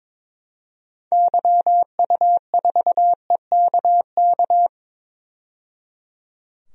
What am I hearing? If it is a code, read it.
Morse code, heard as a keyed tone, about 22 wpm, 700 Hz. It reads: YU4EKK
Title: DXCC-100 Call Signs - 6 Characters - 22wpm